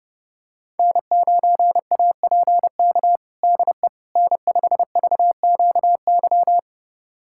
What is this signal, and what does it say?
Morse code, heard as a keyed tone, about 30 wpm, 700 Hz. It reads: N9APK DE N5VQY
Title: USA Callsign DE Callsign - 5 Characters - 30wpm